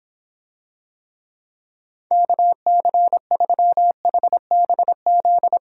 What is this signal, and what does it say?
Morse code, heard as a keyed tone, about 26 wpm, 700 Hz. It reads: KC3HBZ